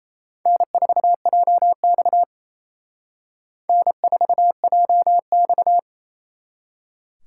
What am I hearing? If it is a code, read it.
Morse code, heard as a keyed tone, about 35 wpm, 700 Hz. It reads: N4JX N4JX